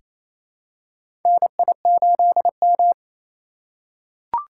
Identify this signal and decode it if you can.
Morse code, heard as a keyed tone, about 27 wpm, 700 Hz. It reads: NI8M
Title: USA Call Signs - 4 Characters - 4x Character Spacing - 28wpm